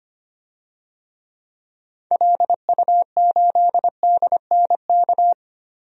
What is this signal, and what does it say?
Morse code, heard as a keyed tone, about 25 wpm, 700 Hz. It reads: LU8DNK
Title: DXCC-100 Call Signs - 6 Characters - 25wpm